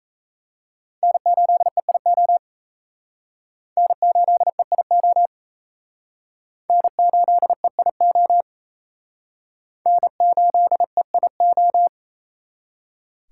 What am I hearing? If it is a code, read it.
Morse code, heard as a keyed tone, about 34 wpm, 700 Hz. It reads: N8EIO N8EIO N8EIO N8EIO
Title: Speed Racing - USA 1x3 Call Signs 28wpm